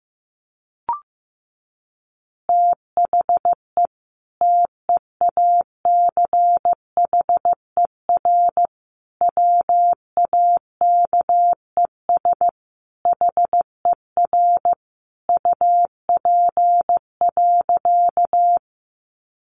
Morse code, heard as a keyed tone, about 15 wpm, 700 Hz. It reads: THE TEACHER WAKES HER UP.
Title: Easy Phrases 15wpm